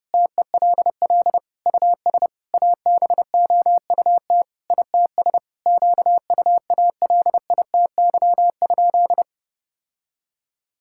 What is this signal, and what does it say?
Morse code, heard as a keyed tone, about 30 wpm, 700 Hz. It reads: TELL US ABOUT ITS QUALITY?
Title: Sentences from Top 1000 Words - Review Part 6 - 30wpm